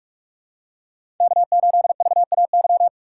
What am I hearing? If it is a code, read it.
Morse code, heard as a keyed tone, about 46 wpm, 700 Hz. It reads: K8VAY
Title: USA Call Signs - 3x Character Spacing - 45wpm